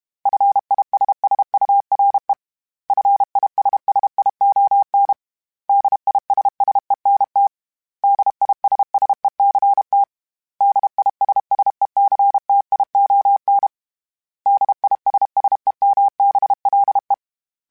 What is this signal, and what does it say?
Morse code, heard as a keyed tone, about 32 wpm, 800 Hz. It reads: FISSURE FISSION DISSENT DISSECT DISSECTION DISSEMBLE
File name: General / General-mp3s / Suffixes @30